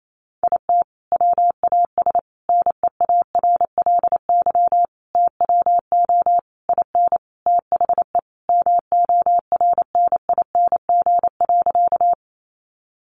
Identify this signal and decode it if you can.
Morse code, heard as a keyed tone, about 28 wpm, 700 Hz. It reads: IT WAS NEARLY TWO IN THE MORNING.
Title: Sentences from Top 1300 Words 28wpm